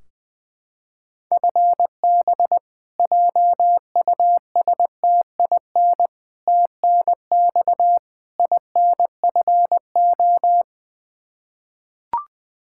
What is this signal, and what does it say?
Morse code, heard as a keyed tone, about 20 wpm, 700 Hz. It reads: FB JUSTIN TNX INFO
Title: Practice QSO - Exchange 3 - More Info - 20wpm